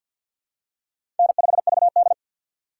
Morse code, heard as a keyed tone, about 47 wpm, 700 Hz. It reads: N5VD
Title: USA Call Signs - 2x Character Spacing - 50wpm